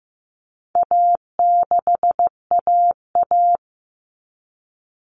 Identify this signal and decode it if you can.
Morse code, heard as a keyed tone, about 15 wpm, 700 Hz. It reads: A6AA
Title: All Cut Numbers - 4-Digit 15wpm